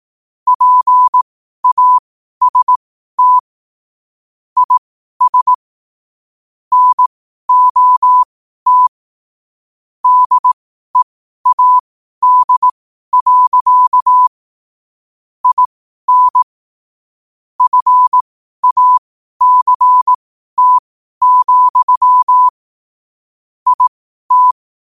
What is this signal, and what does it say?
Morse code, heard as a keyed tone, about 18 wpm, spaced as for 12 wpm, 1000 Hz. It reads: PAST IS NOT DEAD. IN FACT, IT
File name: qotdfiles / 12wpm